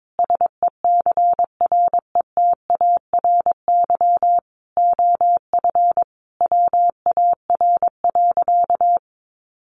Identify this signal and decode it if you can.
Morse code, heard as a keyed tone, about 22 wpm, 700 Hz. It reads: SECRETARY OF WAR.
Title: Sentences from Top 1300 Words 22wpm